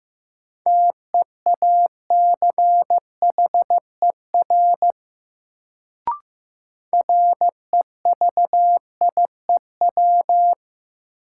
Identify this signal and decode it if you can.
Morse code, heard as a keyed tone, about 15 wpm, 700 Hz. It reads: TEACHER REVIEW
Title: Sentences from Top 900 Words - New Word Drill - 15wpm